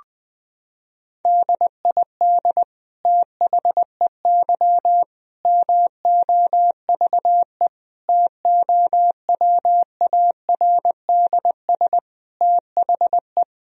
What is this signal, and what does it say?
Morse code, heard as a keyed tone, about 20 wpm, 700 Hz. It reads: DID THEY MOVE TOWARDS THE
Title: Sentences from Top 1000 Words - Review Part 2 - 20wpm